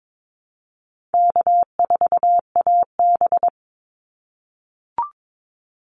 Morse code, heard as a keyed tone, about 22 wpm, 700 Hz. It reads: K4AB
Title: Easy Call Signs - 22wpm